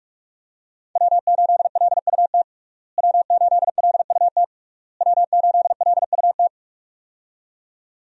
Morse code, heard as a keyed tone, about 41 wpm, 700 Hz. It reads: W8LUT W8LUT W8LUT
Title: USA Call Signs - 5 Characters - Repeated 3x - 45wpm